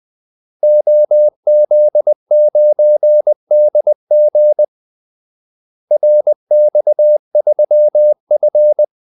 Morse code, heard as a keyed tone, about 20 wpm, 600 Hz. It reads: OZ9DG RX3F
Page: Kaldesignaler 20 wpm